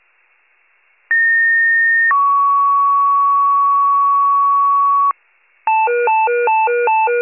Two-tone page, heard as a one-second tone, then a three-second tone, about 1850 then 1120 Hz, with noise. It is heard from 1.1 to 5.1 s.